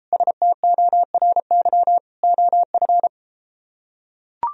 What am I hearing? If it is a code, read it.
Morse code, heard as a keyed tone, about 33 wpm, 700 Hz. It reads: STORY OF